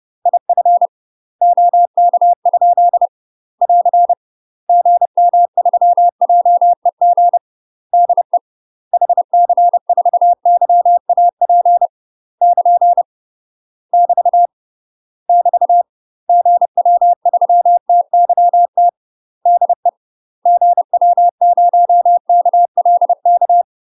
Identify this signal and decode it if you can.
Morse code, heard as a keyed tone, about 30 wpm, 700 Hz. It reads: IF OK? <AR> GM3JEG DE HC4YAP <KN> <BT> <BT> GW3TYT DE GW0KLK
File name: UK MASTER QSOs TRAINING / 30WPM / UK MASTER QSOs 0042